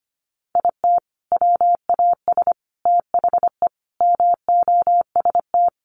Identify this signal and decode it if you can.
Morse code, heard as a keyed tone, about 25 wpm, 700 Hz. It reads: IT WAS THE MOST
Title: Sentences from Top 1400 Words 25wpm